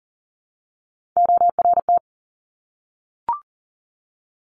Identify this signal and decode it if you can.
Morse code, heard as a keyed tone, about 40 wpm, 700 Hz. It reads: ORT
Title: Top Three Letter Combinations in Words 40wpm